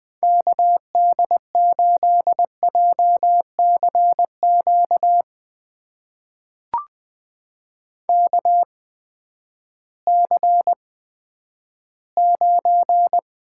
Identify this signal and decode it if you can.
Morse code, heard as a keyed tone, about 20 wpm, spaced as for 11 wpm, 700 Hz. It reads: KD8JCQ KC9
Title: USA Call Signs - 6 Characters - 8x Character Spacing - 20wpm